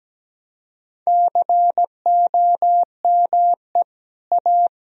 Morse code, heard as a keyed tone, about 17 wpm, 700 Hz. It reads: COME A
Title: Sentences from Top 1000 Words - Review Part 2 - 17wpm